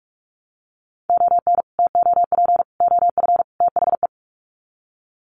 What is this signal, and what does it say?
Morse code, heard as a keyed tone, about 44 wpm, 700 Hz. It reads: ON TOP OF THE
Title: Sets of 4 Words 45wpm